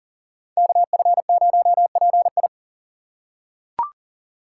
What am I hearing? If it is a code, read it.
Morse code, heard as a keyed tone, about 37 wpm, 700 Hz. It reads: KF0PI
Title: USA Call Signs - 5 Characters - 6x Character Spacing - 40wpm